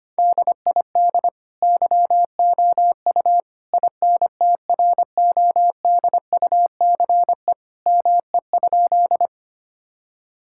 Morse code, heard as a keyed tone, about 25 wpm, 700 Hz. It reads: DID YOU INTRODUCE ME?
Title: Sentences from Top 1600 Words 25wpm